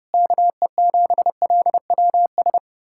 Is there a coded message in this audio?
Yes: Morse code from 0.1 to 2.6 s, heard as a keyed tone, about 33 wpm, 700 Hz. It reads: KE7LWS